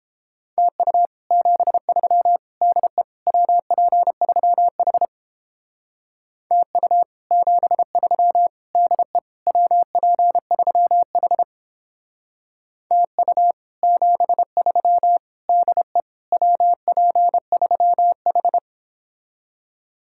Morse code, heard as a keyed tone, about 31 wpm, 700 Hz. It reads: TU 73 DE WP3H TU 73 DE WP3H TU 73 DE WP3H